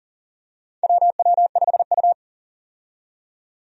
Morse code, heard as a keyed tone, about 40 wpm, 700 Hz. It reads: WW5U
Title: USA Call Signs - 4 Characters - 40wpm